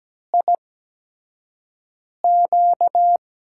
Morse code, heard as a keyed tone, about 17 wpm, spaced as for 5 wpm, 700 Hz. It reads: IQ
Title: USA Call Signs - 6 Characters - 8x Character Spacing - 17wpm